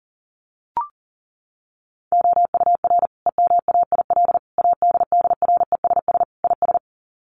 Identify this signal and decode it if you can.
Morse code, heard as a keyed tone, about 40 wpm, 700 Hz. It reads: OUR EMAIL ADDRESS IS
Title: Sets of 5 Words Encore2 40wpm